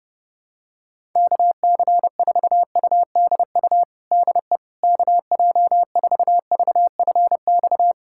Morse code, heard as a keyed tone, about 30 wpm, 700 Hz. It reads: KC4UDU DE KJ4VFX
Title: USA Callsign DE Callsign - 6 Characters - 30wpm